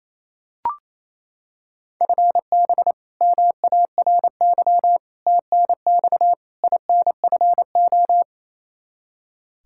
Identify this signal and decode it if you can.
Morse code, heard as a keyed tone, about 27 wpm, 700 Hz. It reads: FB MARY TNX INFO
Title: Practice QSO - Exchange 3 - More Info - 28wpm